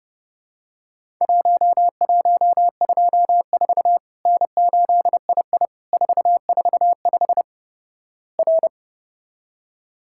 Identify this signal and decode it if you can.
Morse code, heard as a keyed tone, about 30 wpm, 700 Hz. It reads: 1124 N8II 445 R
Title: Worked All Europe DX Contest - QTC Exchange - 30wpm